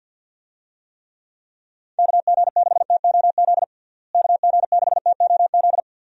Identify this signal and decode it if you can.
Morse code, heard as a keyed tone, about 50 wpm, 700 Hz. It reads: KG6TOZ KG6TOZ
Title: USA Call Signs - 6 Characters - Repeated 2x - 50wpm